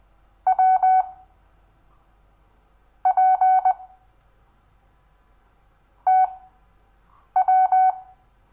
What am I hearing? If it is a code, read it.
Morse code, heard as a keyed tone, about 20 wpm, spaced as for 7 wpm, 750 Hz. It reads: W P TW
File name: cw / Lesson6 WI